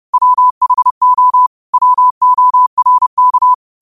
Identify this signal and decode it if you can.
Morse code, heard as a keyed tone, about 30 wpm, 1000 Hz. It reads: WHO WORK